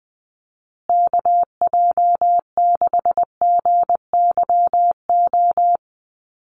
Morse code, heard as a keyed tone, about 20 wpm, 700 Hz. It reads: KJ6GYO
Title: USA Call Signs Repeated 2x 20wpm